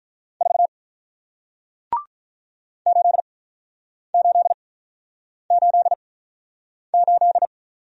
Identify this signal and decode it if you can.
Morse code, heard as a keyed tone, about 48 wpm, spaced as for 9 wpm, 700 Hz. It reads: 4 8888